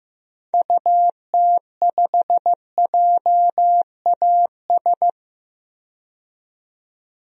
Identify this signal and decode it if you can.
Morse code, heard as a keyed tone, about 15 wpm, 700 Hz. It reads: UT5JAS
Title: DXCC-100 Call Signs - 6 Characters - 15wpm